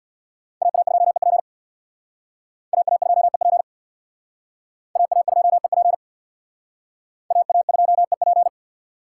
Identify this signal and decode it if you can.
Morse code, heard as a keyed tone, about 62 wpm, 700 Hz. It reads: AA2EP AA2EP AA2EP AA2EP
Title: Speed Racing - USA Call Signs 50wpm